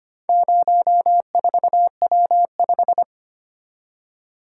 Morse code, heard as a keyed tone, about 25 wpm, 700 Hz. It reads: 04W5
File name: practice / 25wpm / Four Letter-Number Combinations - 25wpm